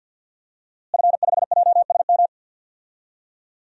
Morse code, heard as a keyed tone, about 52 wpm, 700 Hz. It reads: F5JSM
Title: DXCC-100 Call Signs - 5 Characters - Repeated 3x - 50wpm